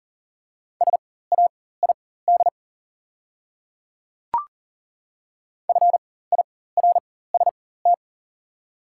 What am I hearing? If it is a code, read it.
Morse code, heard as a keyed tone, about 38 wpm, spaced as for 18 wpm, 700 Hz. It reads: SAID FIRST
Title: 4x Character Spacing - Top 100 Words - 40wpm